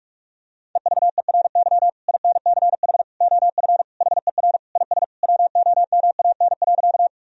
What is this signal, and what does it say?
Morse code, heard as a keyed tone, about 46 wpm, 700 Hz. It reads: EVERY INCH OF HER IS WOMAN.
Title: Sentences from Top 1000 Words - Review Part 3 - 45wpm